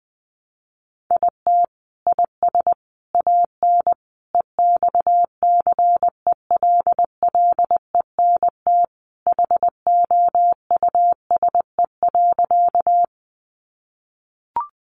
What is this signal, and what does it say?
Morse code, heard as a keyed tone, about 20 wpm, 700 Hz. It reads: IT IS AN EXCELLENT HOUSE.
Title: Sentences from Top 1000 Words 20wpm